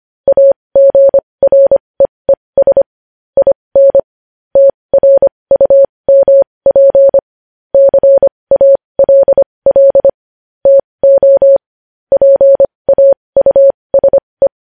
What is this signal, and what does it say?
Morse code, heard as a keyed tone, about 25 wpm, spaced as for 20 wpm, 550 Hz. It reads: AGREES IN TRUMP CALL TO PAUSE
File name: podcast / News headlines-1x-25x20